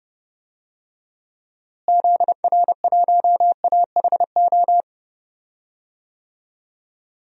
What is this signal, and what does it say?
Morse code, heard as a keyed tone, about 30 wpm, 700 Hz. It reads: ZR1AHO
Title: DXCC-100 Call Signs - 6 Characters - Repeated 2x - 30wpm